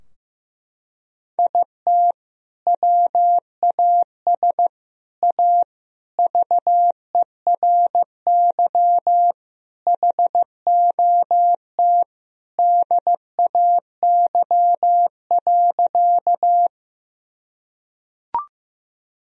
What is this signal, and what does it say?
Morse code, heard as a keyed tone, about 16 wpm, 700 Hz. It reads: IT WAS A VERY HOT DAY.
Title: Sentences from Top 1000 Words - Review Part 3 - 15wpm